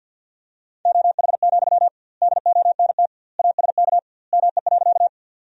Morse code, heard as a keyed tone, about 51 wpm, 700 Hz. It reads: OH, DONT ASK ME.
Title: 2x Word-Spacing - Sentences from Top 500 Words - 50wpm